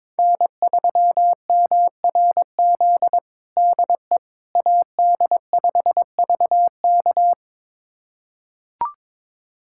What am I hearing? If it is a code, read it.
Morse code, heard as a keyed tone, about 22 wpm, 700 Hz. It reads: N3MRZ DE AD5VK
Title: USA Callsign DE Callsign - 5 Characters - 22wpm